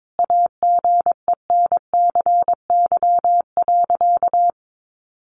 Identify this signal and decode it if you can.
Morse code, heard as a keyed tone, about 22 wpm, 700 Hz. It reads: AGENCY.
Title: Sentences from Top 1300 Words 22wpm